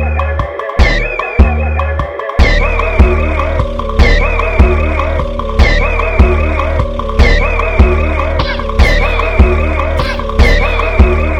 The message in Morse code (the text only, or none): none